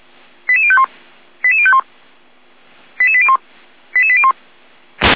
Five-tone selective call (ZVEI), heard as quick runs of five tones, tones of about 70 ms. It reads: 88051 88051 88801 88801